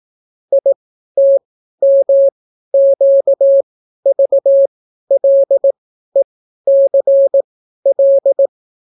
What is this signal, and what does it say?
Morse code, heard as a keyed tone, about 18 wpm, spaced as for 12 wpm, 550 Hz. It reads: ITMQVLECL